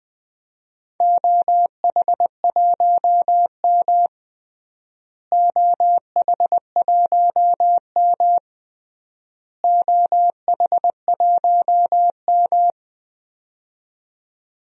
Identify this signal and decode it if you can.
Morse code, heard as a keyed tone, about 20 wpm, 700 Hz. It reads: OH1M OH1M OH1M
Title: DXCC-100 Call Signs - 4 Characters - Repeated 3x - 20wpm